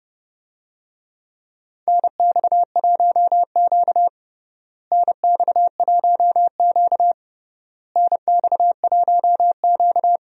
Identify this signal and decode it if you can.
Morse code, heard as a keyed tone, about 30 wpm, 700 Hz. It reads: NX1Q NX1Q NX1Q
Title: USA Call Signs - 4 Characters - Repeated 3x - 30wpm